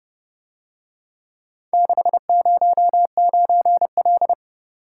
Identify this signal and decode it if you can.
Morse code, heard as a keyed tone, about 29 wpm, 700 Hz. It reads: 609L